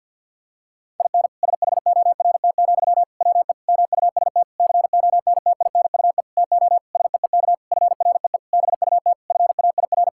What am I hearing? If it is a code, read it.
Morse code, heard as a keyed tone, about 48 wpm, 700 Hz. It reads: IN SHORT, WE MUST CONTINUE TO SEEK FREE BUT FAIR
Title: Sentences from Top 1600 Words 50wpm